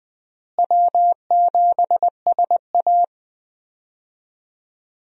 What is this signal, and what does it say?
Morse code, heard as a keyed tone, about 20 wpm, 700 Hz. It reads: W7SA
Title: USA Call Signs - 4 Characters - Repeated 2x - 20wpm